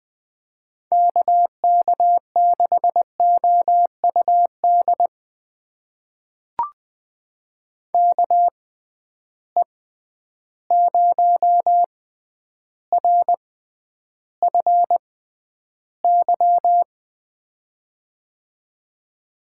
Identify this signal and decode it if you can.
Morse code, heard as a keyed tone, about 20 wpm, spaced as for 10 wpm, 700 Hz. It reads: KK6OUD KE0RFY
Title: USA Call Signs - 6 Characters - 6x Character Spacing - 20wpm